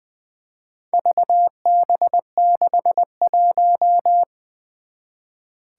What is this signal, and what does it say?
Morse code, heard as a keyed tone, about 20 wpm, 700 Hz. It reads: VB61
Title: Four Challenging Letter-Number Combinations 20wpm